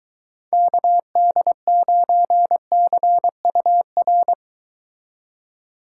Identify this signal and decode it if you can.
Morse code, heard as a keyed tone, about 23 wpm, 700 Hz. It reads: KD9CUR